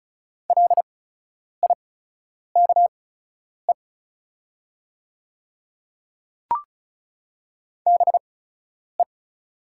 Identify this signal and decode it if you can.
Morse code, heard as a keyed tone, about 36 wpm, spaced as for 10 wpm, 700 Hz. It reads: LIKE BE
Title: 8x Character Spacing - Top 100 Words - 35wpm